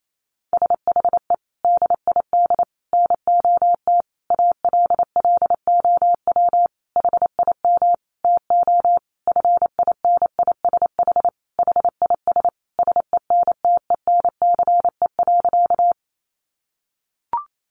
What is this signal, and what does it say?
Morse code, heard as a keyed tone, about 28 wpm, 700 Hz. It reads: SHE DID NOT ALLOW HIM TO FINISH HIS SENTENCE.
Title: Sentences from Top 1200 Words 28wpm